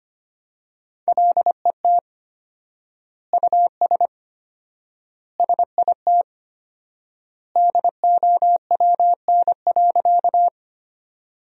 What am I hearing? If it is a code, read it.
Morse code, heard as a keyed tone, about 25 wpm, 700 Hz. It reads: LET US SIT DOWN.
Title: 4x Word-Spacing - Sentences from Top 500 Words - 25wpm